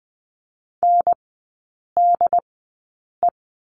Morse code, heard as a keyed tone, about 20 wpm, spaced as for 8 wpm, 700 Hz. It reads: NDE
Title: 067 Lesson 12e - Letter d - Three Characters 20wpm